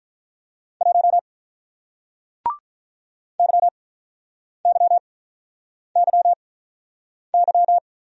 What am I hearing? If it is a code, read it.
Morse code, heard as a keyed tone, about 47 wpm, 700 Hz. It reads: 1 Y Y Y Y